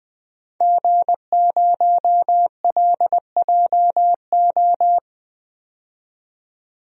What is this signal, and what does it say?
Morse code, heard as a keyed tone, about 21 wpm, 700 Hz. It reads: G0LJO
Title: DXCC-100 Call Signs - 5 Characters - 20wpm